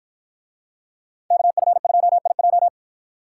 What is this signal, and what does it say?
Morse code, heard as a keyed tone, about 48 wpm, 700 Hz. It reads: KF2IJ